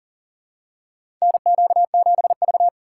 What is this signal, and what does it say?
Morse code, heard as a keyed tone, about 40 wpm, 700 Hz. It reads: NQ7V
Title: USA Call Signs - 4x Character Spacing - 40wpm